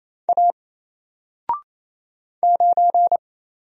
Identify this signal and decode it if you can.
Morse code, heard as a keyed tone, about 27 wpm, 700 Hz. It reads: A 9
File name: Single-Letter-Number-28wpm